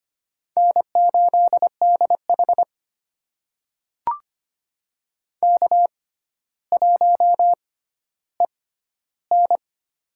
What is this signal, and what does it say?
Morse code, heard as a keyed tone, about 25 wpm, spaced as for 13 wpm, 700 Hz. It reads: N8DH K1EN
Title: USA Call Signs - 6x Character Spacing - 25wpm